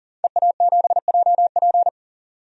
Morse code, heard as a keyed tone, about 40 wpm, 700 Hz. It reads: EA7JP
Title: DXCC-100 Call Signs - 5 Characters - 40wpm